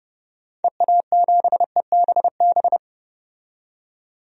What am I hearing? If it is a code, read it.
Morse code, heard as a keyed tone, about 32 wpm, 700 Hz. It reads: EA7EBB
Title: DXCC-100 Call Signs - 6 Characters - 30wpm